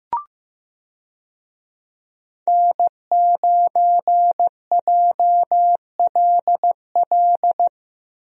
Morse code, heard as a keyed tone, about 15 wpm, 700 Hz. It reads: N9JLL
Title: USA Call Signs - 5 Characters - Repeated 3x - 15wpm